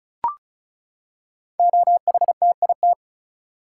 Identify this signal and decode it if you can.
Morse code, heard as a keyed tone, about 36 wpm, 700 Hz. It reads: OHTIT
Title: Code Groups - 5 Letters - 35wpm